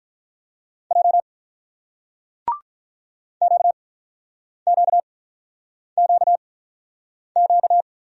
Morse code, heard as a keyed tone, about 44 wpm, spaced as for 8 wpm, 700 Hz. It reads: J QQQQ